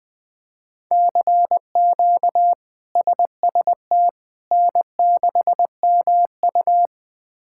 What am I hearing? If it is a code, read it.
Morse code, heard as a keyed tone, about 20 wpm, 700 Hz. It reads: CQ SST N6MU
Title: K1USN SST QSO Practice - 20wpm